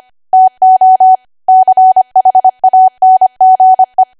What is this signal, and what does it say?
Morse code, heard as a keyed tone, about 25 wpm, 750 Hz. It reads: TO CHANGE
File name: Archive / 25 WPM / 25WPM